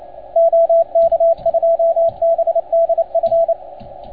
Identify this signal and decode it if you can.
Morse code, heard as a keyed tone, about 28 wpm, 690 Hz, with radio noise. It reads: OK2BDF
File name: Archiv / CUC287 / OK1FGD CUC 287